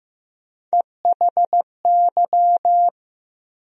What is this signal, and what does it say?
Morse code, heard as a keyed tone, about 15 wpm, 700 Hz. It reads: EHY
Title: USA Call Sign Postfixes - 3 Letters 15wpm